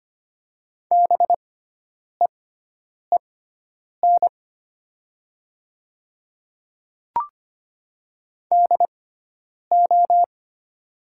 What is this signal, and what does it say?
Morse code, heard as a keyed tone, about 25 wpm, spaced as for 9 wpm, 700 Hz. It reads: BEEN DO
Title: 6x Character Spacing - Top 100 Words - 25wpm